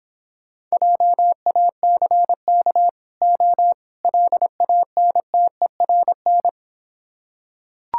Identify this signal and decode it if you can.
Morse code, heard as a keyed tone, about 26 wpm, 700 Hz. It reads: JACK O LANTERN